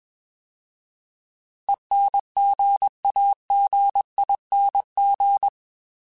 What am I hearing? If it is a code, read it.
Morse code, heard as a keyed tone, about 21 wpm, 800 Hz. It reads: ENGAGING
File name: Alphabetical / Alphabetical mp3s / Es @20